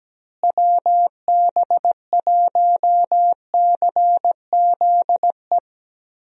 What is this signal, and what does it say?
Morse code, heard as a keyed tone, about 17 wpm, 700 Hz. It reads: WB1CZE
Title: USA Call Signs - 6 Characters - Repeated 3x - 17wpm